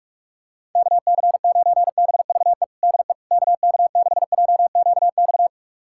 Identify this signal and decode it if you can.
Morse code, heard as a keyed tone, about 46 wpm, 700 Hz. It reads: KC9BVE DE KK6JQX
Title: USA Callsign DE Callsign - 6 Characters - 45wpm